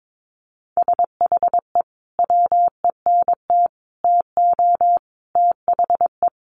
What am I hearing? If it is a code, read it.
Morse code, heard as a keyed tone, about 22 wpm, 700 Hz. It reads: SHE WENT TO THE
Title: Sentences from Top 900 Words 22wpm